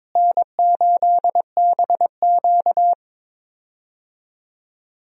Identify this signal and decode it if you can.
Morse code, heard as a keyed tone, about 23 wpm, 700 Hz. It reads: N8BQ
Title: USA Call Signs - 4 Characters - Repeated 3x - 22wpm